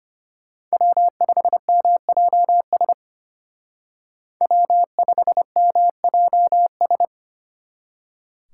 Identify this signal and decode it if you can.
Morse code, heard as a keyed tone, about 30 wpm, 700 Hz. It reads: W5MJS W5MJS
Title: Speed Racing - USA 1x3 Call Signs 25wpm